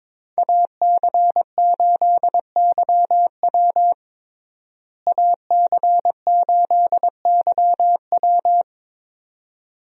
Morse code, heard as a keyed tone, about 22 wpm, 700 Hz. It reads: AC8YW AC8YW
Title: USA Call Signs - 5 Characters - Repeated 2x - 22wpm